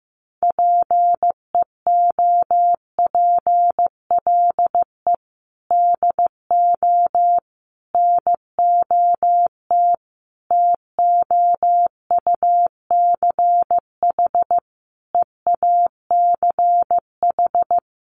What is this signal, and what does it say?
Morse code, heard as a keyed tone, about 14 wpm, 700 Hz. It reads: PEOPLE DO NOT TOUCH EACH
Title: Sentences from Top 1200 Words 15wpm